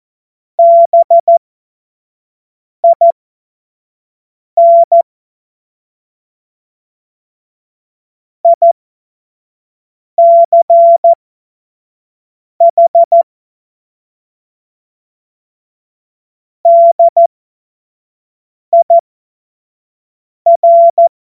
Morse code, heard as a keyed tone, about 14 wpm, spaced as for 5 wpm, 680 Hz. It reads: BIN ICH DIR
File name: Morsekurs / Witze / Witz 28. Dezember 2021, 25 F70